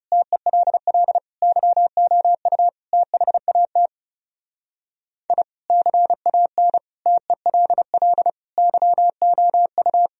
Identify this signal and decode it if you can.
Morse code, heard as a keyed tone, about 38 wpm, 700 Hz. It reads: TELL YOU THAT I CAN TELL YOU